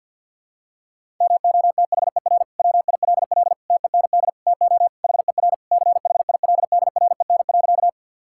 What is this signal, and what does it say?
Morse code, heard as a keyed tone, about 52 wpm, 700 Hz. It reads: MOTHER WILL TEND TO HER CHILDREN.
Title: Sentences from Top 1000 Words - Review Part 5 - 50wpm